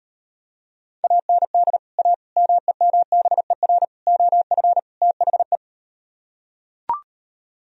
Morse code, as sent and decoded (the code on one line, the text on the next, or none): .- -. -.. / .- / -- . -- -... .-. / --- ..-. / - .... .
AND A MEMBER OF THE